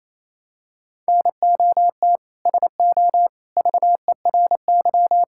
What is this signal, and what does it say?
Morse code, heard as a keyed tone, about 28 wpm, 700 Hz. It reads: NOT SO VERY
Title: Sentences from Top 1600 Words 28wpm